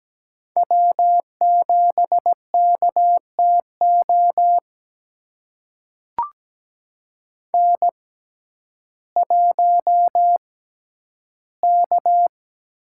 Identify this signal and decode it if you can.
Morse code, heard as a keyed tone, about 17 wpm, spaced as for 10 wpm, 700 Hz. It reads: W7KTO N1K
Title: USA Call Signs - 5 Characters - 6x Character Spacing - 17wpm